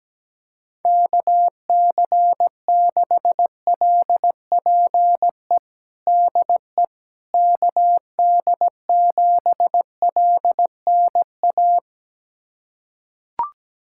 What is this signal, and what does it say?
Morse code, heard as a keyed tone, about 17 wpm, 700 Hz. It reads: KC6LPE DE KD7LNA
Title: USA Callsign DE Callsign - 6 Characters - 17wpm